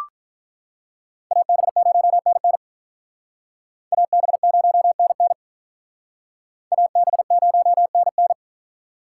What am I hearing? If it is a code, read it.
Morse code, heard as a keyed tone, about 44 wpm, 700 Hz. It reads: AB0NN AB0NN AB0NN
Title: Speed Racing - USA 2x2 Call Signs 35wpm